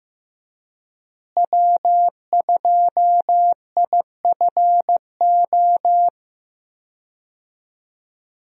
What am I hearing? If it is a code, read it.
Morse code, heard as a keyed tone, about 15 wpm, 700 Hz. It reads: W2IFO
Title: USA Call Signs - 5 Characters - Repeated 2x - 15wpm